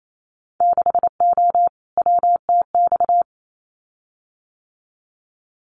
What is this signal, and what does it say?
Morse code, heard as a keyed tone, about 28 wpm, 700 Hz. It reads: 6O WTX
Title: Winter Field Day Exchanges Encore 28wpm